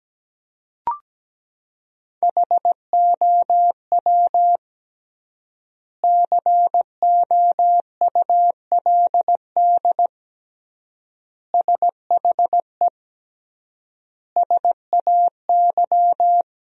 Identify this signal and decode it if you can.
Morse code, heard as a keyed tone, about 17 wpm, 700 Hz. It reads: HOW COULD SHE SAY
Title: 3x Word-Spacing - Sentences from Top 200 Words - 17wpm